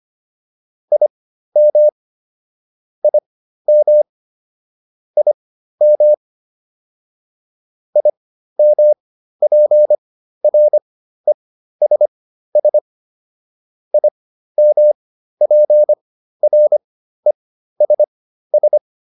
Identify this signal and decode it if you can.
Morse code, heard as a keyed tone, about 25 wpm, spaced as for 13 wpm, 600 Hz. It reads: IM IM IM IMPRESS IMPRESS
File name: IM7 13